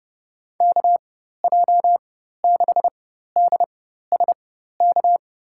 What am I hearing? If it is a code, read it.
Morse code, heard as a keyed tone, about 30 wpm, spaced as for 14 wpm, 700 Hz. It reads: KJ6DSK
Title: USA Call Signs - 6 Characters - 4x Character Spacing - 30wpm